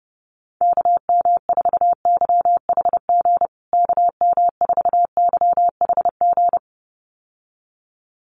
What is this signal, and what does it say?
Morse code, heard as a keyed tone, about 30 wpm, 700 Hz. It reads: KM4YHG KM4YHG
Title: USA Call Signs Repeated 2x 30wpm